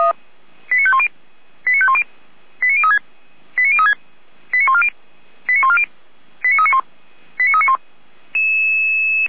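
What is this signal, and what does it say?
Five-tone selective call (ZVEI), heard as quick runs of five tones, tones of about 70 ms. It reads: 89510 89510 89036 89036 89140 89140 89391 89391
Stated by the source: ZVEI five-tone selective call 89125 received by SDR.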